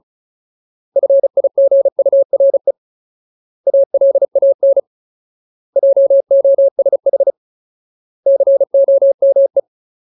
Morse code, heard as a keyed tone, about 34 wpm, 550 Hz. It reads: FIGURE ALAN JOSH COME
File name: Ultimate Word List / 35WPM Ultimate